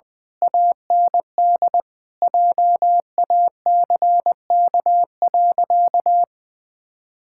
Morse code, heard as a keyed tone, about 21 wpm, 700 Hz. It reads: AND JACK.
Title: Pangram Challenge v2 - 20wpm